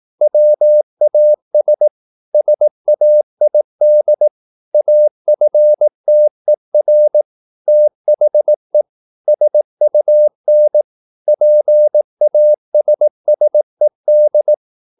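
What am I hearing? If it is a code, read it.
Morse code, heard as a keyed tone, about 18 wpm, 600 Hz. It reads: WAS SAID AFTER THE SUN PASSED
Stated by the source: Morse practice at 18 wpm, English text, from CW med Gnister.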